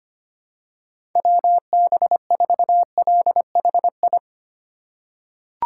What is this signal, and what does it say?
Morse code, heard as a keyed tone, about 25 wpm, 700 Hz. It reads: WB4LHI
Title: USA Call Signs - 6 Characters - 4x Character Spacing - 25wpm